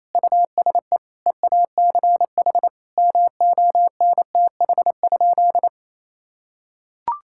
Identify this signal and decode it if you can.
Morse code, heard as a keyed tone, about 28 wpm, 700 Hz. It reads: USE EACH MONTH?
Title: Sentences from Top 1000 Words - Review Part 6 - 28wpm